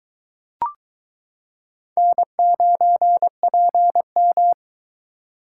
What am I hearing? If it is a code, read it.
Morse code, heard as a keyed tone, about 22 wpm, 700 Hz. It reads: N9PM